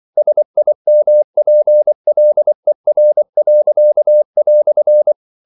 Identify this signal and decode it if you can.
Morse code, heard as a keyed tone, about 24 wpm, 600 Hz. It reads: SIMPLER."
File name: CW / Practice / EnglishQuotations0224